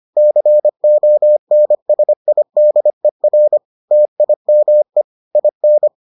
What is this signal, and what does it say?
Morse code, heard as a keyed tone, about 25 wpm, 600 Hz. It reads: CONSIDER TIME IN